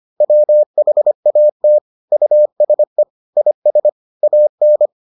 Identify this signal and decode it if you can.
Morse code, heard as a keyed tone, about 25 wpm, 600 Hz. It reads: WHAT USE IS AN